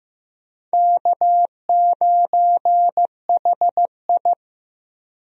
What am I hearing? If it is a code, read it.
Morse code, heard as a keyed tone, about 16 wpm, 700 Hz. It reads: K9HI
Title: USA Call Signs - 4 Characters - Repeated 2x - 15wpm